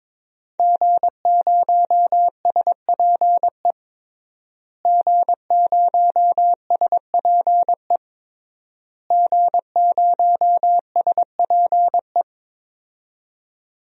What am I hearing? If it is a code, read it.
Morse code, heard as a keyed tone, about 22 wpm, 700 Hz. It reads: G0SPE G0SPE G0SPE
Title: DXCC-100 Call Signs - 5 Characters - Repeated 3x - 22wpm